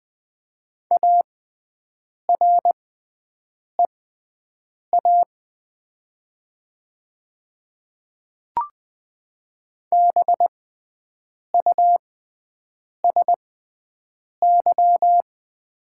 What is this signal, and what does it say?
Morse code, heard as a keyed tone, about 20 wpm, spaced as for 7 wpm, 700 Hz. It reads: AREA BUSY
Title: 6x Character Spacing - Top 500 Words - 20wpm